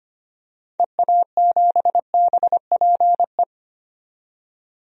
Morse code, heard as a keyed tone, about 25 wpm, 700 Hz. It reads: EA7BPE
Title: DXCC-100 Call Signs - 6 Characters - 25wpm